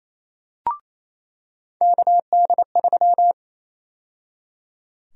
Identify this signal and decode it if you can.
Morse code, heard as a keyed tone, about 28 wpm, 700 Hz. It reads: KD3
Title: USA Call Sign Prefixes - 28wpm